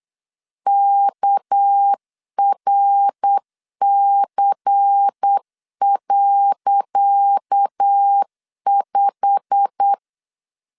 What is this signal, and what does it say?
Morse code, heard as a keyed tone, about 8 wpm, 800 Hz. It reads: KRC.5
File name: LEZIONI / 10 SIGNS / 10C DOT 8wpm 3649